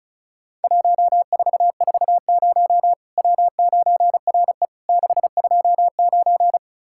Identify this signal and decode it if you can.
Morse code, heard as a keyed tone, about 35 wpm, 700 Hz. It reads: 1440 W9RE 629
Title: Worked All Europe DX Contest - QTC Exchange - 35wpm